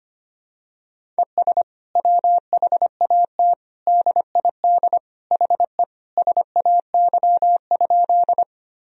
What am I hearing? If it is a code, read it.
Morse code, heard as a keyed tone, about 25 wpm, 700 Hz. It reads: ES WHAT DID HE SAY?
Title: Sentences from Top 200 Words and Abbreviations 25wpm